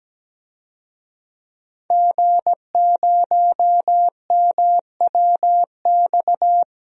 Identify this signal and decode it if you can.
Morse code, heard as a keyed tone, about 17 wpm, 700 Hz. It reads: G0MWX
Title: DXCC-100 Call Signs - 5 Characters - Repeated 2x - 17wpm